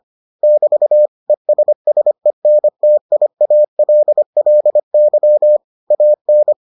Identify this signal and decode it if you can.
Morse code, heard as a keyed tone, about 25 wpm, 600 Hz. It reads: <BT> ESSENTIALLY AN